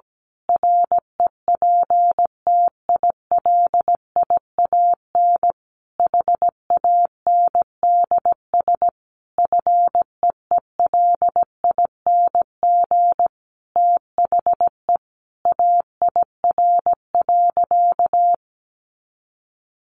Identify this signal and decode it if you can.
Morse code, heard as a keyed tone, about 17 wpm, 700 Hz. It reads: REPTILIAN HANDS FEELING THE AIR.
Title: A Sound Of Thunder 17wpm